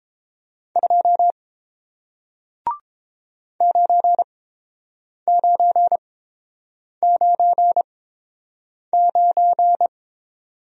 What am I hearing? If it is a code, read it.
Morse code, heard as a keyed tone, about 30 wpm, spaced as for 7 wpm, 700 Hz. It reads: 2 9999